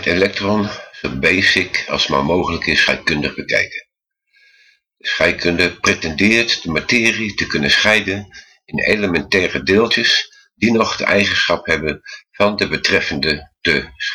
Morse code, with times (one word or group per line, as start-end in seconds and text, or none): none